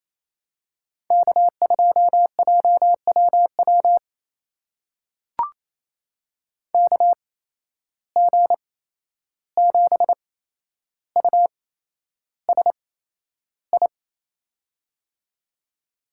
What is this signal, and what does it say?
Morse code, heard as a keyed tone, about 28 wpm, spaced as for 11 wpm, 700 Hz. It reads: K2JWW KG7USI
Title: USA Call Signs - 8x Character Spacing - 28wpm